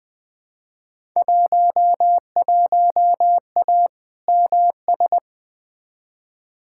Morse code, heard as a keyed tone, about 21 wpm, 700 Hz. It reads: 11A MS
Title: Field Day Exchanges Encore - 20wpm